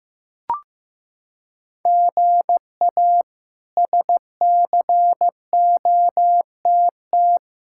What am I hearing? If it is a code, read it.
Morse code, heard as a keyed tone, about 15 wpm, 700 Hz. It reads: GA SCOTT
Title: K1USN SST QSO Practice - 15wpm